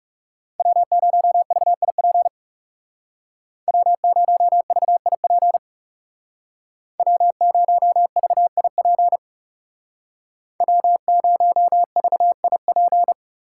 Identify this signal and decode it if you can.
Morse code, heard as a keyed tone, about 36 wpm, 700 Hz. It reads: W0VIP W0VIP W0VIP W0VIP
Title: Speed Racing - USA 1x3 Call Signs 30wpm